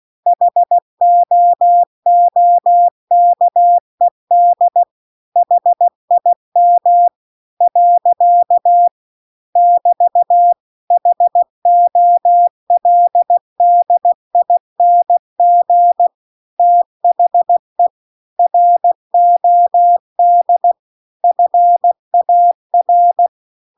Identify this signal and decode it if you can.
Morse code, heard as a keyed tone, about 16 wpm, 700 Hz. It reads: HOOKED HIM . <BT> HOLDING THE ROD FAR